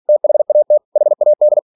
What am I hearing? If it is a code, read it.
Morse code, heard as a keyed tone, about 46 wpm, 600 Hz. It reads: THAT HAD